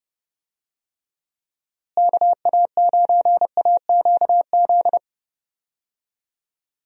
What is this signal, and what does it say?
Morse code, heard as a keyed tone, about 31 wpm, 700 Hz. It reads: KA9AQZ
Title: USA Call Signs - 6 Characters - Repeated 3x - 30wpm